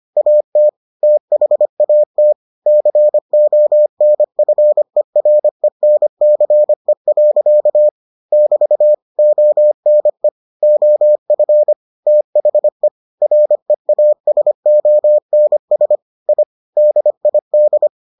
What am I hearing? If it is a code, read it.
Morse code, heard as a keyed tone, about 25 wpm, 600 Hz. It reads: AT THAT CONFERENCE. <BT> ONE OF THE REASONS I DID